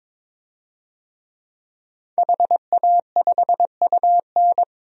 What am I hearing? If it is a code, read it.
Morse code, heard as a keyed tone, about 22 wpm, 700 Hz. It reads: HA5UN
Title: DXCC-100 Call Signs - 5 Characters - 22wpm